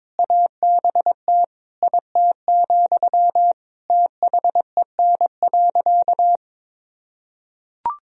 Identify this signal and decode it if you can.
Morse code, heard as a keyed tone, about 22 wpm, 700 Hz. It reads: ABT IT, THEN.
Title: Sentences from Top 200 Words and Abbreviations 22wpm